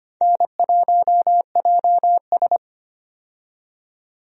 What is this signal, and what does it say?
Morse code, heard as a keyed tone, about 25 wpm, 700 Hz. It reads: N1JS
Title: USA Call Signs - 4 Characters - 25wpm